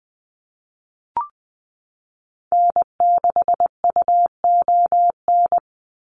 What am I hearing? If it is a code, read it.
Morse code, heard as a keyed tone, about 20 wpm, 700 Hz. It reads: N6UON
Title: USA Call Signs 20wpm